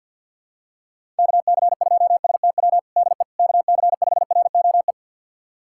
Morse code, heard as a keyed tone, about 46 wpm, 700 Hz. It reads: KC2STW DE KC5ROE
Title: USA Callsign DE Callsign - 6 Characters - 50wpm